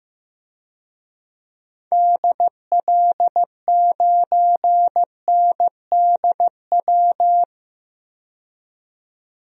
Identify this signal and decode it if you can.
Morse code, heard as a keyed tone, about 15 wpm, 700 Hz. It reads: DL9NDW
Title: DXCC-100 Call Signs - 6 Characters - 15wpm